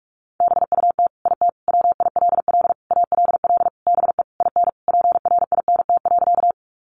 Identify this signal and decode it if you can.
Morse code, heard as a keyed tone, about 44 wpm, 700 Hz. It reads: BUT IT WILL ALL BE IN PRINT.
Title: Sentences from Top 1400 Words 45wpm